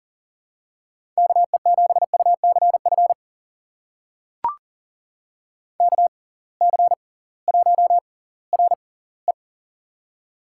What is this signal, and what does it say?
Morse code, heard as a keyed tone, about 40 wpm, spaced as for 22 wpm, 700 Hz. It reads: KE7UCF KC1RE
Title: USA Call Signs - 6x Character Spacing - 40wpm